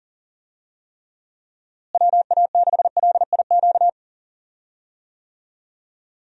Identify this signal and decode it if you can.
Morse code, heard as a keyed tone, about 40 wpm, 700 Hz. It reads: WA6LIQ
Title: USA Call Signs - 6 Characters - Repeated 2x - 40wpm